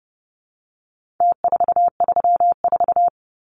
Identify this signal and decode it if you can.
Morse code, heard as a keyed tone, about 30 wpm, 700 Hz. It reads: T434
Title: Numbers - Contest Style - 4-Digit 30wpm